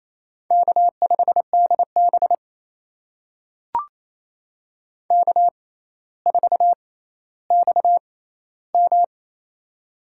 Morse code, heard as a keyed tone, about 28 wpm, spaced as for 14 wpm, 700 Hz. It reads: K5DB K4XM
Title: USA Call Signs - 4 Characters - 6x Character Spacing - 28wpm